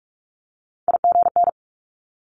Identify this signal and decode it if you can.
Morse code, heard as a keyed tone, about 46 wpm, 690 Hz. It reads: IGN
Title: Multi-letter Phonemes 45wpm